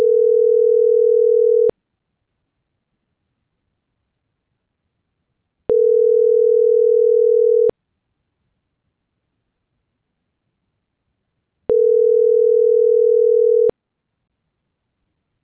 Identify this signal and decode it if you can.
Ringback tone (North America), 440 + 480 Hz.